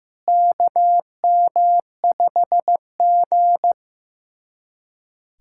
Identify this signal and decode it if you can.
Morse code, heard as a keyed tone, about 16 wpm, 700 Hz. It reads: KM5G
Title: USA Callsign DE Callsign - 4 Characters - 15wpm